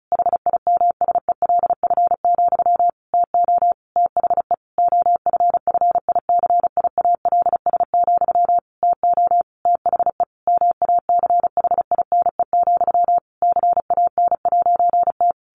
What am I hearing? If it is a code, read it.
Morse code, heard as a keyed tone, about 34 wpm, 700 Hz. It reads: HIMSELF, TO THE OFFICIALS, TO THE MACHINE, CAN'T
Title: A Sound Of Thunder 35wpm